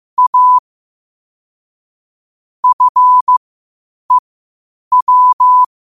Morse code, heard as a keyed tone, about 15 wpm, spaced as for 8 wpm, 1000 Hz. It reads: A FEW
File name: qotdfiles / 8wpm